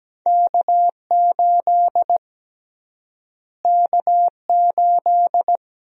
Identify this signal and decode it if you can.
Morse code, heard as a keyed tone, about 17 wpm, 700 Hz. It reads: K8 K8
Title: USA Call Sign Prefixes - Repeated 2x - 17wpm